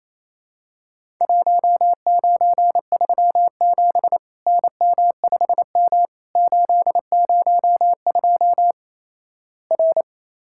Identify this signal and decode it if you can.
Morse code, heard as a keyed tone, about 28 wpm, 700 Hz. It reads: 1937 NM5M 802 R
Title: Worked All Europe DX Contest - QTC Exchange - 28wpm